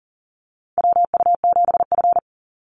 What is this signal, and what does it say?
Morse code, heard as a keyed tone, about 40 wpm, 700 Hz. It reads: WU7F
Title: Call Signs - 40wpm